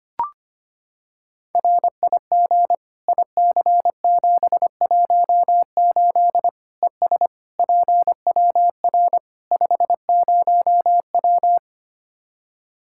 Morse code, heard as a keyed tone, about 25 wpm, 700 Hz. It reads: RIG IC718 ES PWR 50W
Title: Practice QSO - Exchange 2 - Info - 25wpm